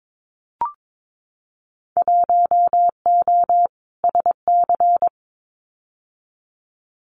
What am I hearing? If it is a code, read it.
Morse code, heard as a keyed tone, about 22 wpm, 700 Hz. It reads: 1O SC
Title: Winter Field Day Exchanges 22wpm